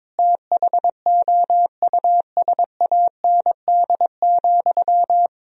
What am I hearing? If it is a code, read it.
Morse code, heard as a keyed tone, about 23 wpm, 700 Hz. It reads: THOUSAND,
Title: Sentences from Top 1000 Words - Review Part 3 - 22wpm